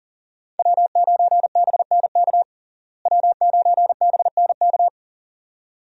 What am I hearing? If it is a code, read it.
Morse code, heard as a keyed tone, about 37 wpm, 700 Hz. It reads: W9BNK W9BNK
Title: USA Call Signs - 5 Characters - Repeated 2x - 40wpm